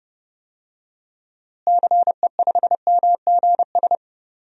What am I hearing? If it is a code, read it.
Morse code, heard as a keyed tone, about 30 wpm, 700 Hz. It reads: CE5MGS
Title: DXCC-100 Call Signs - 6 Characters - Repeated 3x - 30wpm